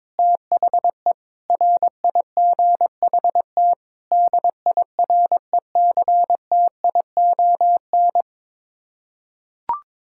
Morse code, heard as a keyed tone, about 22 wpm, 700 Hz. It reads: THE RIGHT DIRECTION
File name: practice / 22wpm / Sets of 8 Words - 22wpm